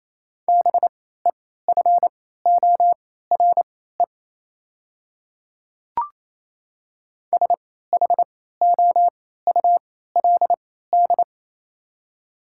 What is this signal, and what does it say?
Morse code, heard as a keyed tone, about 27 wpm, spaced as for 16 wpm, 700 Hz. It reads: BEFORE SHOULD
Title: 3x Character Spacing - Top 300 Words - 28wpm